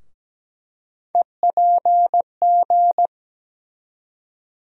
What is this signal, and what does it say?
Morse code, heard as a keyed tone, about 17 wpm, 700 Hz. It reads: EPG
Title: USA Call Sign Postfixes - 3 Letters 17wpm